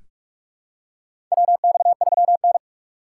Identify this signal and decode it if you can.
Morse code, heard as a keyed tone, about 46 wpm, 700 Hz. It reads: WX3N
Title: USA Call Signs - 4x Character Spacing - 45wpm